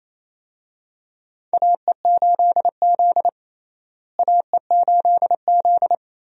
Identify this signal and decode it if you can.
Morse code, heard as a keyed tone, about 28 wpm, 700 Hz. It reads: AE8Z AE8Z
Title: USA Call Signs - 4 Characters - Repeated 2x - 28wpm